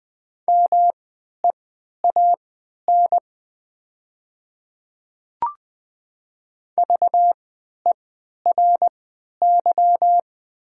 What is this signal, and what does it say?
Morse code, heard as a keyed tone, about 20 wpm, spaced as for 11 wpm, 700 Hz. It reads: MEAN VERY
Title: 3x Character Spacing - Top 200 Words - 20wpm